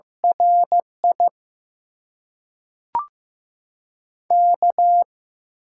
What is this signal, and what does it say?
Morse code, heard as a keyed tone, about 15 wpm, 700 Hz. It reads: RI K
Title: USA Call Signs - 6 Characters - 8x Character Spacing - 15wpm